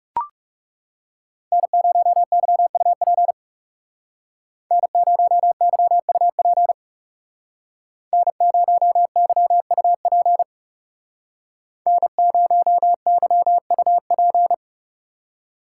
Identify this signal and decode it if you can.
Morse code, heard as a keyed tone, about 37 wpm, 700 Hz. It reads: N0YUP N0YUP N0YUP N0YUP
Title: Speed Racing - USA 1x3 Call Signs 30wpm